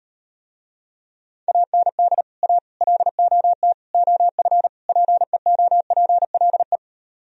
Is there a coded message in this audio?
Yes: Morse code from 1.5 to 6.8 s, heard as a keyed tone, about 38 wpm, 700 Hz. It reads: AND A LOT OF PEOPLE